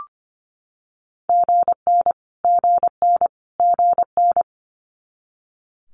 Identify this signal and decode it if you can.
Morse code, heard as a keyed tone, about 25 wpm, 700 Hz. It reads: GN GN GN
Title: Multi-letter Phonemes Repeated 3x 25wpm